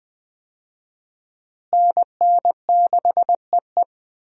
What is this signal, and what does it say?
Morse code, heard as a keyed tone, about 21 wpm, 700 Hz. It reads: NN6EE
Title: USA Call Signs - 5 Characters - Repeated 2x - 20wpm